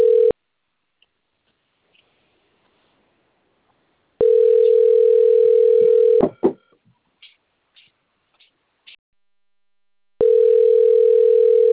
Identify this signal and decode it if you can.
Ringback tone (North America), 440 + 480 Hz.